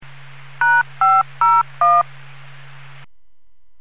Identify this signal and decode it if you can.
DTMF, heard as telephone touch tones, tones of about 220 ms. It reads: #501